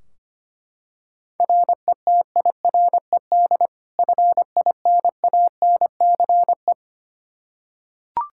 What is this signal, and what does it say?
Morse code, heard as a keyed tone, about 25 wpm, 700 Hz. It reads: RETIRED FINANCE
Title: Practice QSO - Exchange 3 - More Info - 25wpm